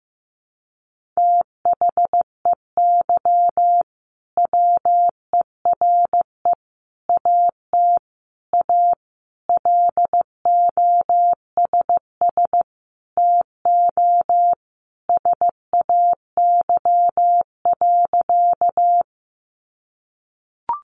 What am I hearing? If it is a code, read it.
Morse code, heard as a keyed tone, about 15 wpm, 700 Hz. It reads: THEY WERE AT A LOSS TO SAY.
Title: Sentences from Top 1400 Words 15wpm